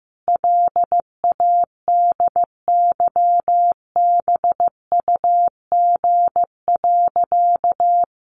Morse code, heard as a keyed tone, about 15 wpm, 700 Hz. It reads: LADYBUG.
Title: Additional Easy Phrases 15wpm